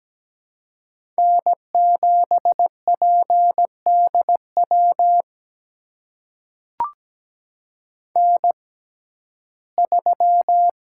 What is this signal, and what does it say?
Morse code, heard as a keyed tone, about 17 wpm, spaced as for 12 wpm, 700 Hz. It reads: N7PDW N3
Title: USA Call Signs - 5 Characters - 6x Character Spacing - 17wpm